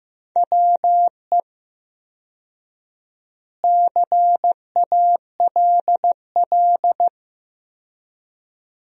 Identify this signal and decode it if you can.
Morse code, heard as a keyed tone, about 15 wpm, 700 Hz. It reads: WE CALL
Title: 4x Word-Spacing - Sentences from Top 400 Words - 15wpm